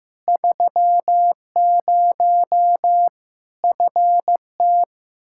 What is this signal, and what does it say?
Morse code, heard as a keyed tone, about 15 wpm, 700 Hz. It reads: 30 FT
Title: K1USN SST QSO Practice 15wpm